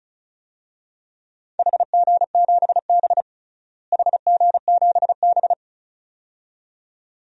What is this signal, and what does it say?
Morse code, heard as a keyed tone, about 34 wpm, 700 Hz. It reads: HG7B HG7B
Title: DXCC-100 Call Signs - 4 Characters - Repeated 2x - 35wpm